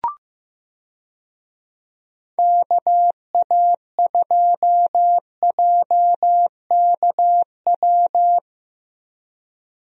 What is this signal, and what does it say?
Morse code, heard as a keyed tone, about 15 wpm, 700 Hz. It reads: KA2JKW
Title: USA Call Signs - 6 Characters - Repeated 3x - 15wpm